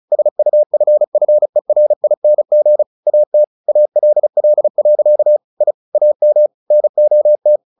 Morse code, heard as a keyed tone, about 34 wpm, 600 Hz. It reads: SUFFERING AT ALL. I AM NOT
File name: the old man and the sea / the old man and the sea 1 35wpm